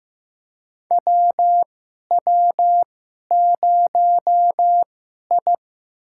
Morse code, heard as a keyed tone, about 15 wpm, spaced as for 11 wpm, 700 Hz. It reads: WW0I
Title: USA Call Signs - 4 Characters - 2x Character Spacing - 15wpm